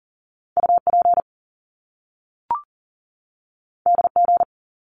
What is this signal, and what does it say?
Morse code, heard as a keyed tone, about 38 wpm, 700 Hz. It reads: UP DG
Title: Top Two Letter Combinations in Words 40wpm